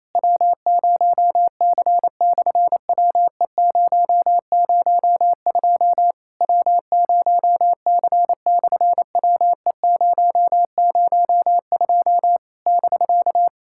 Morse code, heard as a keyed tone, about 28 wpm, 700 Hz. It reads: W0C/WE002 W0C/WE002 <BK>
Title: SOTA QSO Practice 28wpm